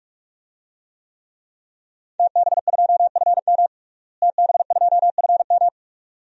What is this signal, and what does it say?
Morse code, heard as a keyed tone, about 46 wpm, 700 Hz. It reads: TB2FM TB2FM
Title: DXCC-100 Call Signs - 5 Characters - Repeated 2x - 45wpm